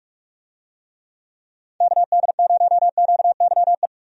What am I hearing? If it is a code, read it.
Morse code, heard as a keyed tone, about 42 wpm, 700 Hz. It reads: KD0QYE
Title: USA Call Signs - 6 Characters - 45wpm